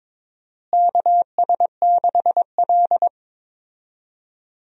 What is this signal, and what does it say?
Morse code, heard as a keyed tone, about 22 wpm, 700 Hz. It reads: KS6L
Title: USA Call Signs - 4 Characters - Repeated 2x - 22wpm